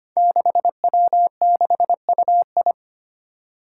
Morse code, heard as a keyed tone, about 25 wpm, 700 Hz. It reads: 6W6UI
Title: Five Challenging Letter-Number Combinations 25wpm